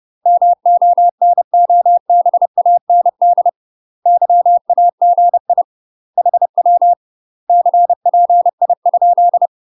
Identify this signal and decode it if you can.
Morse code, heard as a keyed tone, about 30 wpm, 700 Hz. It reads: MONOBAND YAGI HW CPI?
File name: UK MASTER QSOs TRAINING / 30WPM / UK MASTER QSOs 0047